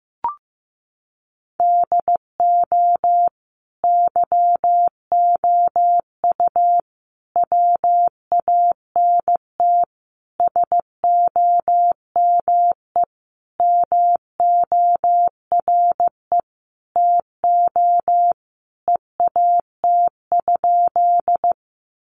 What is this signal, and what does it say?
Morse code, heard as a keyed tone, about 15 wpm, 700 Hz. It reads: DO YOU WANT SOME MORE TO EAT?
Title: Sentences from Top 300 Words 15wpm